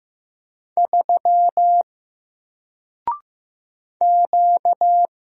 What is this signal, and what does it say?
Morse code, heard as a keyed tone, about 15 wpm, 700 Hz. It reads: 3 Q
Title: Single Character (Letter:Number:Punctuation:Prosign) 15wpm